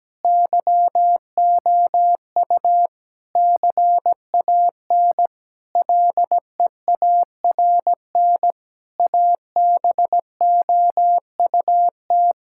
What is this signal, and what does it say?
Morse code, heard as a keyed tone, about 17 wpm, 700 Hz. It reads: YOU CAN LEARN ABOUT
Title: Sentences from Top 1000 Words - Review Part 5 - 17wpm